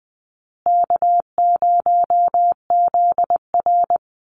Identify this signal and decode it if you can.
Morse code, heard as a keyed tone, about 20 wpm, 700 Hz. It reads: K0ZR
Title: Easy Call Signs Repeated 2x - 20wpm